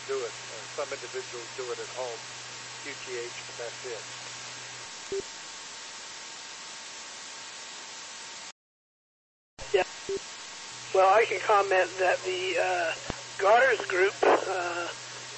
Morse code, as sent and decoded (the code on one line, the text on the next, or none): none